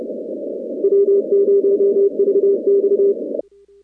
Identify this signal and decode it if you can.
Morse code, heard as a keyed tone, about 27 wpm, 410 Hz. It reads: W0VX
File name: contests / 2019-cw-wpx-kt5j-k5pi- -2330 0-41